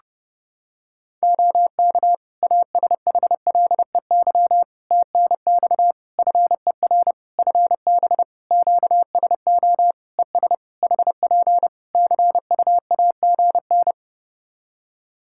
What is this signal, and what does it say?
Morse code, heard as a keyed tone, about 30 wpm, 700 Hz. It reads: OK ASHLEY TNX FER FB QSO ES HP CUAGN
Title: Practice QSO - Exchange 4 - Ending - 30wpm